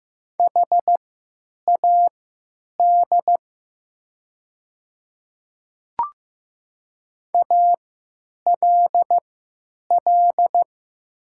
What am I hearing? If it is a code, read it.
Morse code, heard as a keyed tone, about 15 wpm, spaced as for 9 wpm, 690 Hz. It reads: HAD ALL
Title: 3x Character Spacing - Top 100 Words - 15wpm